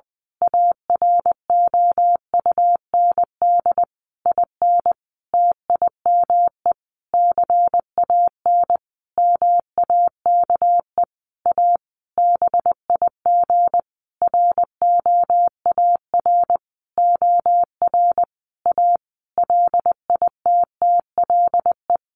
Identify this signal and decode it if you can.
Morse code, heard as a keyed tone, about 19 wpm, 700 Hz. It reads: AROUND IN TIME CAN MAKE A BIG ROAR OR A LITTLE
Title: A Sound Of Thunder 20wpm